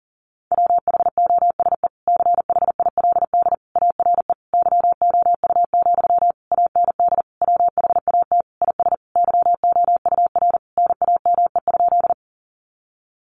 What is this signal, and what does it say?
Morse code, heard as a keyed tone, about 40 wpm, 700 Hz. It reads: WHOSE CHILD ARE YOU, AND WHAT IS YOUR NAME?
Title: Sentences from Top 1200 Words 40wpm